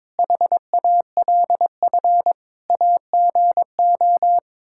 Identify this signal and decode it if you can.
Morse code, heard as a keyed tone, about 22 wpm, 700 Hz. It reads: HALF AGO
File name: practice / 22wpm / Sets of 6 Words - 22wpm